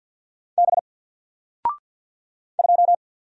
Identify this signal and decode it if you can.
Morse code, heard as a keyed tone, about 48 wpm, 700 Hz. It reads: B 2
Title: Single Character (Letter:Number:Punctuation:Prosign) Alt 50wpm